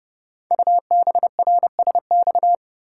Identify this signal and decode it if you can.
Morse code, heard as a keyed tone, about 30 wpm, 700 Hz. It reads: UBRSX
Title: Code Groups - 5 Letters - 30wpm